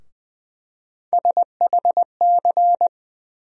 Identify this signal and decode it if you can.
Morse code, heard as a keyed tone, about 20 wpm, 700 Hz. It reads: SHC
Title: USA Call Sign Postfixes - 3 Letters 20wpm